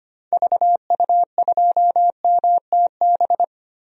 Morse code, heard as a keyed tone, about 25 wpm, 700 Hz. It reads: VU2MTB
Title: DXCC-100 Call Signs - 6 Characters - Repeated 3x - 25wpm